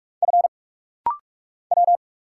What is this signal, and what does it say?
Morse code, heard as a keyed tone, about 46 wpm, 700 Hz. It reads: F W
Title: Single Letters - Rapid-Fire 45wpm